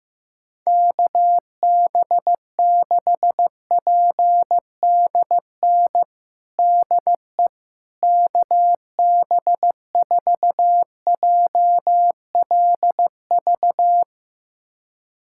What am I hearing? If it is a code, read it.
Morse code, heard as a keyed tone, about 15 wpm, 700 Hz. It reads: KB6PDN DE KB4JLV
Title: USA Callsign DE Callsign - 6 Characters - 15wpm